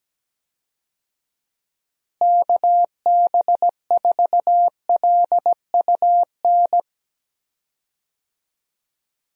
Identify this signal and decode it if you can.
Morse code, heard as a keyed tone, about 17 wpm, 700 Hz. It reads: KB4LUN
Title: USA Call Signs - 6 Characters - Repeated 2x - 17wpm